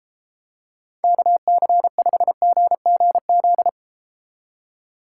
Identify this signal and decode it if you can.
Morse code, heard as a keyed tone, about 30 wpm, 700 Hz. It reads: KC5GGZ